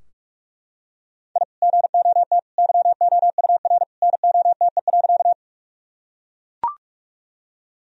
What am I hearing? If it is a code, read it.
Morse code, heard as a keyed tone, about 41 wpm, 700 Hz. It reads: I GOT YOUR NOTE.
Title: Sentences from Top 1000 Words - Review Part 2 - 45wpm